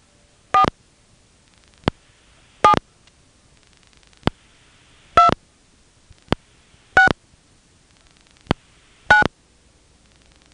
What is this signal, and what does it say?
DTMF, heard as telephone touch tones, tones of about 100 ms. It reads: 4 7 2 6 9
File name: phonetrips / Nantes1HQ